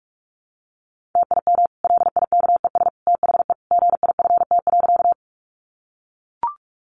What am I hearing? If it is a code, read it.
Morse code, heard as a keyed tone, about 44 wpm, 700 Hz. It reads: TIM LIKES THE GIFT.
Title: Easy Phrases 45wpm